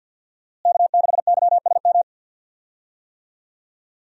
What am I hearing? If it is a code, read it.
Morse code, heard as a keyed tone, about 48 wpm, 700 Hz. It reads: K6YSM
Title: USA Call Signs - 5 Characters - Repeated 3x - 50wpm